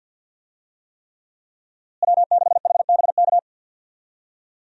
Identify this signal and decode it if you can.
Morse code, heard as a keyed tone, about 46 wpm, 700 Hz. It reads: W6HBK
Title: USA Call Signs - 5 Characters - Repeated 3x - 50wpm